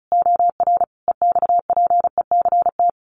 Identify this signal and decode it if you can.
Morse code, heard as a keyed tone, about 34 wpm, 700 Hz. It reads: OR EXPECT
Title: Sentences from Top 1400 Words 35wpm